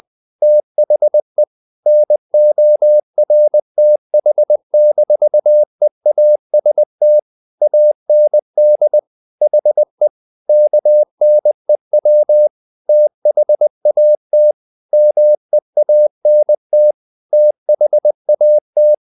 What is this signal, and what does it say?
Morse code, heard as a keyed tone, about 20 wpm, 600 Hz. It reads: THE NORTH-EAST AND HE KNEW THAT MEANT THAT
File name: the old man and the sea part 2 20wpm